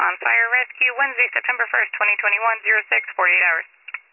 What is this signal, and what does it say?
Emergency Alert System (SAME) tones, heard as harsh buzzing bursts, with speech. It is heard from 0.2 to 3.1 s.